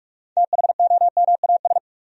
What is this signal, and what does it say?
Morse code, heard as a keyed tone, about 44 wpm, 700 Hz. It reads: THOMAS
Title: USA Common Names - 45wpm